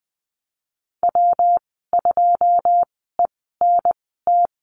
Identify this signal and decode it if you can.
Morse code, heard as a keyed tone, about 20 wpm, spaced as for 14 wpm, 700 Hz. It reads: W2ENT
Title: 128 Lesson 21g - Number 6 - Call Signs 20wpm